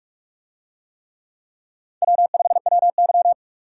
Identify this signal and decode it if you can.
Morse code, heard as a keyed tone, about 44 wpm, 700 Hz. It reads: W5WY